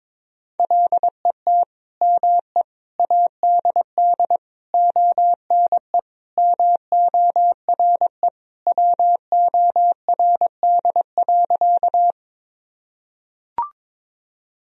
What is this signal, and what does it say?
Morse code, heard as a keyed tone, about 22 wpm, 700 Hz. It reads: LET ME ADD ONE MORE WORD.
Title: Sentences from Top 1000 Words - Review Part 1 - 22wpm